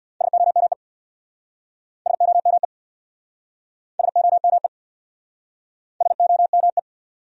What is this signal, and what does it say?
Morse code, heard as a keyed tone, about 58 wpm, 700 Hz. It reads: SOME SOME SOME SOME
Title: Speed-Racing Top 100 Words 50wpm